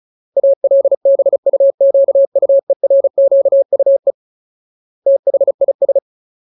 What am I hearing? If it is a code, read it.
Morse code, heard as a keyed tone, about 34 wpm, 550 Hz. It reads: ALBUQUERQUE THIS
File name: Ultimate Word List / 35WPM Ultimate